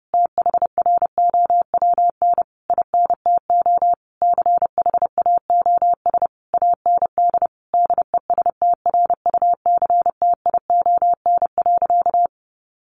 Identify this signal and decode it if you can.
Morse code, heard as a keyed tone, about 30 wpm, 700 Hz. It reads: THROWN INTO CHAOS AND DESTRUCTION.
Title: A Sound Of Thunder 30wpm